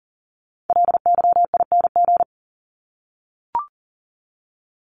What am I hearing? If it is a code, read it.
Morse code, heard as a keyed tone, about 40 wpm, 700 Hz. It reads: LYING
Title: ING Suffix - Top 100 Words with Base Words 40wpm